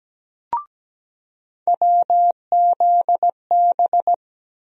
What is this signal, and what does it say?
Morse code, heard as a keyed tone, about 17 wpm, 700 Hz. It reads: WZB
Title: USA Call Sign Postfixes - 3 Letters 17wpm